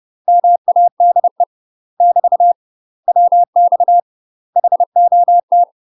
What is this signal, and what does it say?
Morse code, heard as a keyed tone, about 30 wpm, 700 Hz. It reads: MADE <BT> WX HOT
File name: UK MASTER QSOs TRAINING / 30WPM / UK MASTER QSOs 0015